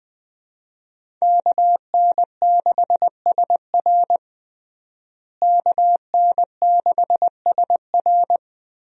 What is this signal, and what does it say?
Morse code, heard as a keyed tone, about 20 wpm, 700 Hz. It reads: KN6SR KN6SR
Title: USA Call Signs - 5 Characters - Repeated 2x - 20wpm